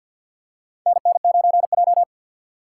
Morse code, heard as a keyed tone, about 46 wpm, 700 Hz. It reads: NN9J
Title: USA Call Signs - 4 Characters - 2x Character Spacing - 50wpm